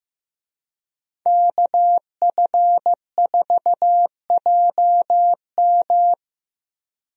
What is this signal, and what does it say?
Morse code, heard as a keyed tone, about 15 wpm, 700 Hz. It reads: KF4JM
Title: USA Call Signs - 5 Characters - Repeated 3x - 15wpm